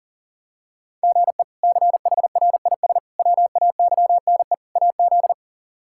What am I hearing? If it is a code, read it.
Morse code, heard as a keyed tone, about 40 wpm, 700 Hz. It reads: GE CHRIS WAYNE AZ
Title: K1USN SST QSO Practice - 40wpm